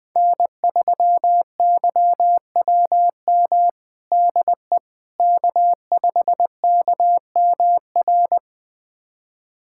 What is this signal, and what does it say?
Morse code, heard as a keyed tone, about 20 wpm, 700 Hz. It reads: N3YWM DE K5KMR
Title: USA Callsign DE Callsign - 5 Characters - 20wpm